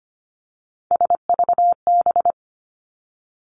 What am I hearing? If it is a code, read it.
Morse code, heard as a keyed tone, about 25 wpm, 700 Hz. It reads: SVB
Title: Three Challenging Letter-Number Combinations 25wpm